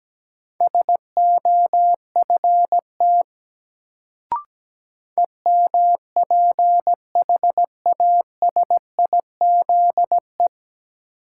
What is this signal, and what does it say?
Morse code, heard as a keyed tone, about 17 wpm, 700 Hz. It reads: SOFT EMPHASIZE
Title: Sentences from Top 1500 Words - New Word Drill - 17wpm